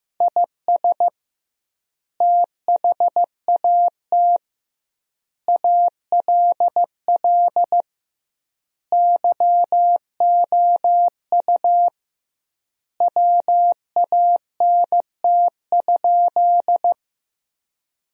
Morse code, heard as a keyed tone, about 15 wpm, 700 Hz. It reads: IS THAT ALL YOU WANT?
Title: 2x Word-Spacing - Sentences from Top 200 Words - 15wpm